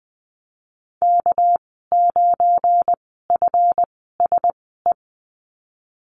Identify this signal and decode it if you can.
Morse code, heard as a keyed tone, about 20 wpm, spaced as for 15 wpm, 700 Hz. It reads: K9FSE
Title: 219 Lesson 35g - Number 0 - Call Signs 20wpm